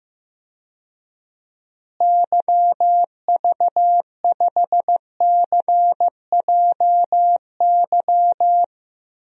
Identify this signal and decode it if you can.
Morse code, heard as a keyed tone, about 15 wpm, 700 Hz. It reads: YV5CJY
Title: DXCC-100 Call Signs - 6 Characters - Repeated 2x - 15wpm